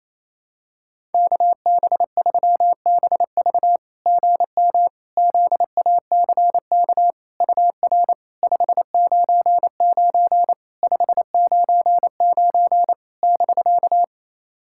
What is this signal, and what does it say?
Morse code, heard as a keyed tone, about 28 wpm, 700 Hz. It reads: KB3BV GM ZACK UR 599 599 <BK>
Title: SOTA QSO Practice 28wpm